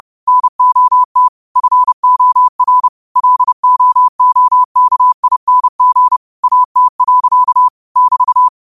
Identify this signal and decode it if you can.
Morse code, heard as a keyed tone, about 30 wpm, 1000 Hz. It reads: NOT FOR LOOKING AT. <BT>
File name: qotdfiles / 30wpm